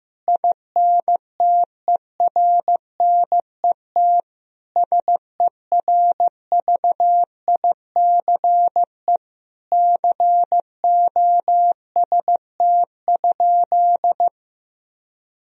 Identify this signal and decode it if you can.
Morse code, heard as a keyed tone, about 15 wpm, 700 Hz. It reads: INTERNET SERVICE COST?
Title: Sentences from Top 1000 Words - Review Part 4 - 15wpm